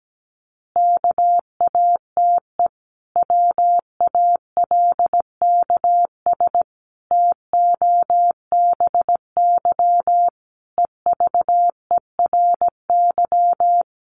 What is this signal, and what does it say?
Morse code, heard as a keyed tone, about 17 wpm, 700 Hz. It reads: KATE WALKS TOBY EVERY
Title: Easy Phrases 17wpm